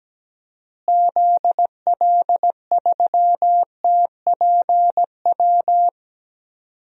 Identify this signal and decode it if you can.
Morse code, heard as a keyed tone, about 17 wpm, 700 Hz. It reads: ZL3TPW
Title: DXCC-100 Call Signs - 6 Characters - Repeated 3x - 17wpm